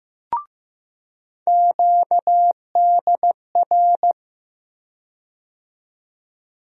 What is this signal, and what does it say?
Morse code, heard as a keyed tone, about 15 wpm, 700 Hz. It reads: QDR
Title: USA Call Sign Postfixes - 3 Letters 15wpm